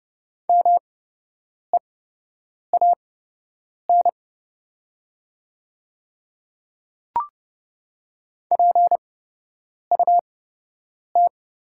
Morse code, heard as a keyed tone, about 32 wpm, spaced as for 8 wpm, 700 Hz. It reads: MEAN PUT
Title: 8x Character Spacing - Top 200 Words - 30wpm